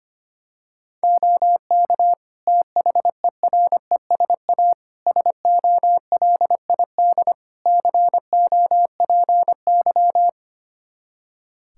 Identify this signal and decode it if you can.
Morse code, heard as a keyed tone, about 25 wpm, 700 Hz. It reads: OK THERESA SOLID COPY
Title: Practice QSO - Exchange 3 - More Info - 25wpm